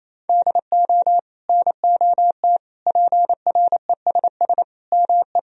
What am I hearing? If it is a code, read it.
Morse code, heard as a keyed tone, about 28 wpm, 700 Hz. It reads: DO NOT PRESS ME
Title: Sentences from Top 1000 Words - Review Part 1 - 28wpm